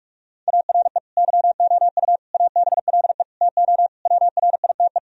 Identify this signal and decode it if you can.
Morse code, heard as a keyed tone, about 41 wpm, 700 Hz. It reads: ARE YOU ABLE TO WRITE
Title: Sentences from Top 1000 Words - Review Part 3 - 45wpm